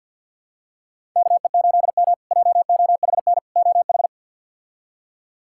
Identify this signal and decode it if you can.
Morse code, heard as a keyed tone, about 48 wpm, 700 Hz. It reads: KE8M JOHN OH
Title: K1USN SST QSO Practice - 50wpm